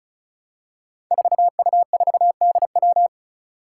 Morse code, heard as a keyed tone, about 36 wpm, 700 Hz. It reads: 4U4DW
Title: Five Challenging Letter-Number Combinations 35wpm